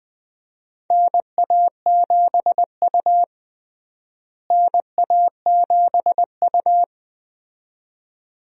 Morse code, heard as a keyed tone, about 21 wpm, 700 Hz. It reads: NA7U NA7U
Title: USA Call Signs - 4 Characters - Repeated 2x - 20wpm